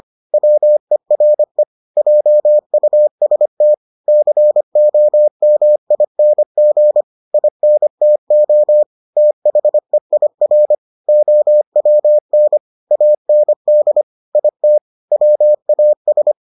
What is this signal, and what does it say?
Morse code, heard as a keyed tone, about 25 wpm, 600 Hz. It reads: WERE JUST COMING INTO THEIR OWN AND IT WAS